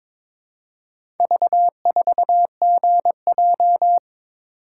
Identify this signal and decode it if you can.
Morse code, heard as a keyed tone, about 23 wpm, 700 Hz. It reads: V4GJ
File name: practice / 22wpm / Four Letter-Number Combinations - 22wpm